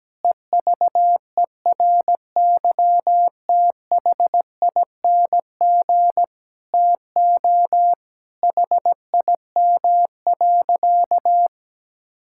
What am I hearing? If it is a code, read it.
Morse code, heard as a keyed tone, about 17 wpm, 700 Hz. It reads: EVERYTHING TO HIM.
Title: Sentences from Top 1000 Words - Review Part 6 - 17wpm